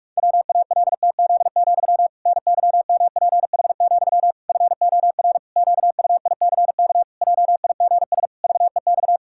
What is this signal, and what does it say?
Morse code, heard as a keyed tone, about 46 wpm, 700 Hz. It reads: WALTZ, NYMPH, FOR QUICK JIGS VEX
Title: Pangram Challenge v2 - 45wpm